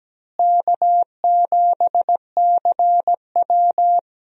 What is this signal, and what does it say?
Morse code, heard as a keyed tone, about 17 wpm, 700 Hz. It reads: K7CW
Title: USA Call Signs - 4 Characters - Repeated 2x - 17wpm